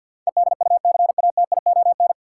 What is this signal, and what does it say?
Morse code, heard as a keyed tone, about 52 wpm, 700 Hz. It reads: EDUCATION
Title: Sentences from Top 1100 Words - New Word Drill Alt - 50wpm